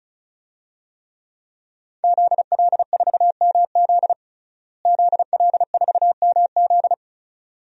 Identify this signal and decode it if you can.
Morse code, heard as a keyed tone, about 36 wpm, 700 Hz. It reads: ZL4MZ ZL4MZ
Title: DXCC-100 Call Signs - 5 Characters - Repeated 2x - 35wpm